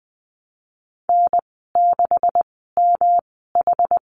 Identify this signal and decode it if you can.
Morse code, heard as a keyed tone, about 20 wpm, spaced as for 15 wpm, 700 Hz. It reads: N6MH
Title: 161 Lesson 26g - Letter g - Call Signs 20wpm